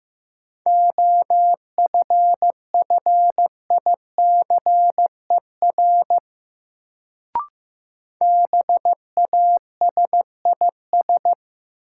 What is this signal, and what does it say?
Morse code, heard as a keyed tone, about 15 wpm, 700 Hz. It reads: OFFICER BASIS
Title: Sentences from Top 1200 Words - New Word Drill Alt - 15wpm